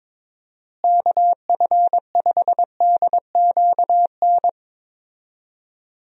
Morse code, heard as a keyed tone, about 22 wpm, 700 Hz. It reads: KF5DQN
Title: USA Call Signs - 6 Characters - Repeated 2x - 22wpm